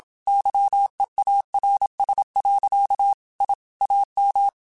Morse code, heard as a keyed tone, about 27 wpm, 800 Hz. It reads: YEARS. I AM
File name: QSOs / QSOs-mp3s / WA7QOX 2@25